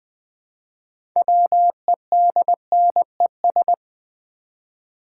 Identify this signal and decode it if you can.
Morse code, heard as a keyed tone, about 20 wpm, 700 Hz. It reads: WEDNES
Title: Build A Word - 3 Letters At A Time - 9-Letter Words - 20wpm